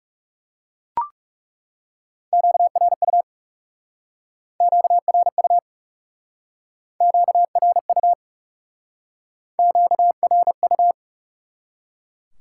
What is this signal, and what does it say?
Morse code, heard as a keyed tone, about 39 wpm, 700 Hz. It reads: QRU QRU QRU QRU